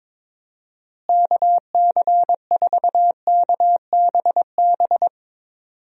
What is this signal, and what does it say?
Morse code, heard as a keyed tone, about 22 wpm, 700 Hz. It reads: KC4KBB
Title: USA Call Signs - 6 Characters - Repeated 3x - 22wpm